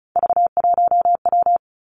Morse code, heard as a keyed tone, about 36 wpm, 700 Hz. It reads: V1W
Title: Three Challenging Letter-Number Combinations 35wpm